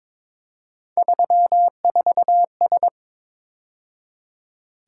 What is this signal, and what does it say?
Morse code, heard as a keyed tone, about 26 wpm, 700 Hz. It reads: 34S
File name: Speed-Racing - Three Letter-Number Combinations - 22wpm